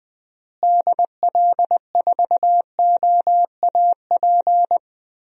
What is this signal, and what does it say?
Morse code, heard as a keyed tone, about 20 wpm, 700 Hz. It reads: DL4OAP
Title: USA Call Signs - 6 Characters - Repeated 3x - 20wpm